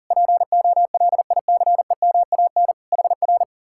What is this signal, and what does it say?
Morse code, heard as a keyed tone, about 52 wpm, 700 Hz. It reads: POLICEMAN HR